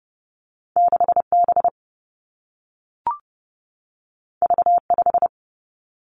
Morse code, heard as a keyed tone, about 30 wpm, 700 Hz. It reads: 6 B V 5
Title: Two Challenging Letter-Number Combinations 30wpm